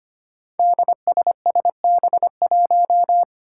Morse code, heard as a keyed tone, about 25 wpm, 700 Hz. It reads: DSSB1
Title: Five Challenging Letter-Number Combinations 25wpm